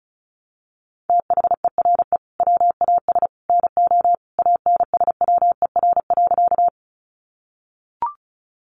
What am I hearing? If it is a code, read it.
Morse code, heard as a keyed tone, about 34 wpm, 700 Hz. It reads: THERE WAS NO ANSWER.
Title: Sentences from Top 300 Words 35wpm